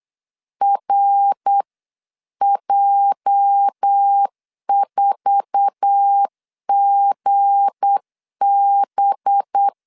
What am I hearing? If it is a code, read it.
Morse code, heard as a keyed tone, about 8 wpm, 800 Hz. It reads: RJ4GB
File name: CW / DET / 1 9 1500C S B 8wpm 5624